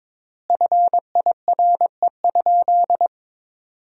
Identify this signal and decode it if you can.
Morse code, heard as a keyed tone, about 22 wpm, 700 Hz. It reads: FIRE?
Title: 3x Word-Spacing - Sentences from Top 400 Words - 22wpm